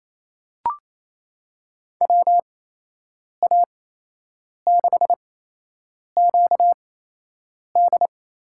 Morse code, heard as a keyed tone, about 27 wpm, 700 Hz. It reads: W A 6 Q D
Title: USA Call Signs - 6 Characters - 8x Character Spacing - 28wpm